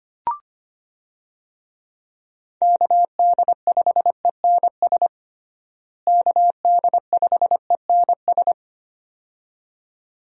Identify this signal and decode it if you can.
Morse code, heard as a keyed tone, about 25 wpm, 700 Hz. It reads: KD5ENS KD5ENS
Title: USA Call Signs - 6 Characters - Repeated 2x - 25wpm